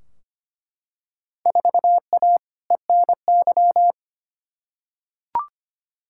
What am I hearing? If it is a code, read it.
Morse code, heard as a keyed tone, about 25 wpm, 700 Hz. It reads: 4A ENY
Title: Field Day Exchanges - 25wpm